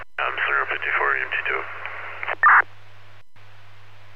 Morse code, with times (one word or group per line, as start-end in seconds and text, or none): none